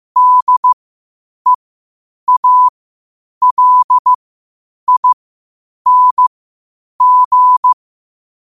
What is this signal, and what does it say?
Morse code, heard as a keyed tone, about 15 wpm, spaced as for 8 wpm, 1000 Hz. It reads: DEALING
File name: qotdfiles / 8wpm